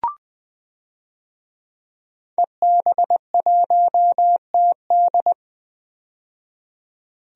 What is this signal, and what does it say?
Morse code, heard as a keyed tone, about 20 wpm, 700 Hz. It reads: EB1TD
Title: DXCC-100 Call Signs - 5 Characters - 20wpm